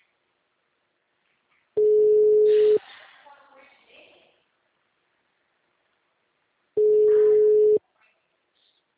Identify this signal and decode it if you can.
Ringback tone (Europe, 425 Hz), with speech.